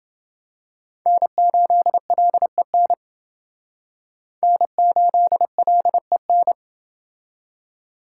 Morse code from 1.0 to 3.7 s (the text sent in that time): N8LEN